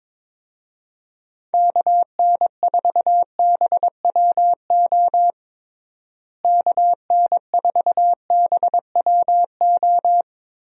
Morse code, heard as a keyed tone, about 22 wpm, 700 Hz. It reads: KN4BWO KN4BWO
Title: USA Call Signs - 6 Characters - Repeated 2x - 22wpm